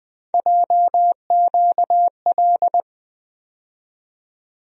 Morse code, heard as a keyed tone, about 20 wpm, 700 Hz. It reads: JQL
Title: USA Call Sign Postfixes - 3 Letters 20wpm